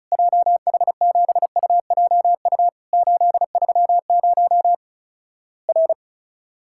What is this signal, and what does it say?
Morse code, heard as a keyed tone, about 36 wpm, 700 Hz. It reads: JH7UJU 830 R
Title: Worked All Europe DX Contest - QTC Exchange - 35wpm